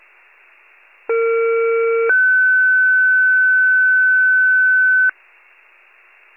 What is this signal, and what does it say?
Two-tone page, heard as a one-second tone, then a three-second tone, about 460 then 1500 Hz. It is heard from 1.1 to 5.1 s.
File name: pager / 21 FIRE / wcalert